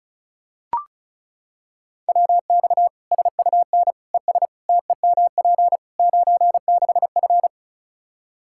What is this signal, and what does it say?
Morse code, heard as a keyed tone, about 34 wpm, 700 Hz. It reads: WX SUN ES TEMP 96F
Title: Practice QSO - Exchange 2 - Info - 35wpm